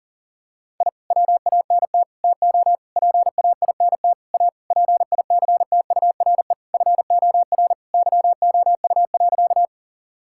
Morse code, heard as a keyed tone, about 37 wpm, 700 Hz. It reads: I WANT TO PAINT A PICTURE FOR YOU.
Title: Sentences from Top 1000 Words - Review Part 6 - 40wpm